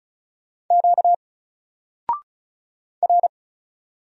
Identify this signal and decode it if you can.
Morse code, heard as a keyed tone, about 36 wpm, 700 Hz. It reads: Q R
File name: Single-Letter-Number-35wpm